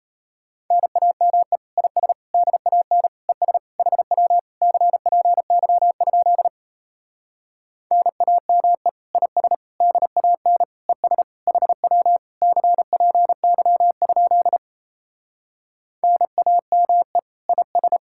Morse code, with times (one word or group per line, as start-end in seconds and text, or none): 0.7-1.6 NAME
1.8-2.1 IS
2.3-3.1 DAN
3.3-3.6 ES
3.8-4.4 HW
4.6-6.5 CPY?
7.9-8.9 NAME
9.1-9.5 IS
9.8-10.6 DAN
10.9-11.2 ES
11.5-12.2 HW
12.4-14.6 CPY?
16.0-17.2 NAME
17.5-18.0 IS